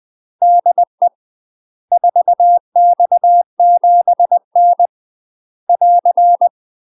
Morse code, heard as a keyed tone, about 20 wpm, 700 Hz. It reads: DE 4X7N <AR>
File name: UK MASTER QSOs TRAINING / 20WPM / UK MASTER QSOs 0037